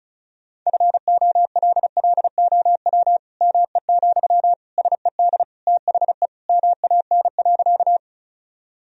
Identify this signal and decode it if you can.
Morse code, heard as a keyed tone, about 36 wpm, 700 Hz. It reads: FOLLOW ME, SED THE MAN.
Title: Sentences from Top 200 Words and Abbreviations 35wpm